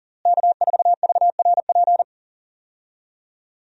Morse code, heard as a keyed tone, about 37 wpm, 700 Hz. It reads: K4VRP
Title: USA Call Signs - 5 Characters - Repeated 2x - 40wpm